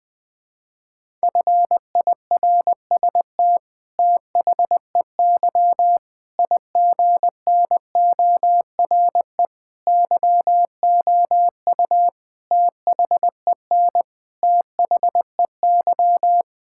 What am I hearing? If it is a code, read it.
Morse code, heard as a keyed tone, about 20 wpm, 700 Hz. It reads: FIRST THEY IGNORE YOU THEN THEY
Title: Sentences with 8-Word Phrases - Spoken Beginning - 20wpm